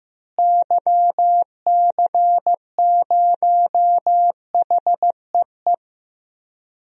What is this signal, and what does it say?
Morse code, heard as a keyed tone, about 15 wpm, 700 Hz. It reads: YC0HEE
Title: DXCC-100 Call Signs - 6 Characters - Repeated 2x - 15wpm